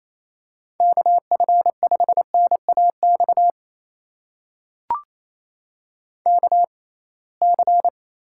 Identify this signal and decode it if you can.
Morse code, heard as a keyed tone, about 28 wpm, spaced as for 21 wpm, 700 Hz. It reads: KF5NAX KC
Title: USA Call Signs - 6 Characters - 6x Character Spacing - 28wpm